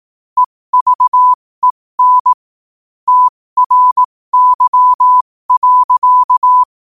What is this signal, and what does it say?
Morse code, heard as a keyed tone, about 18 wpm, spaced as for 15 wpm, 1000 Hz. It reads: EVEN TRY.
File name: qotdfiles / 15wpm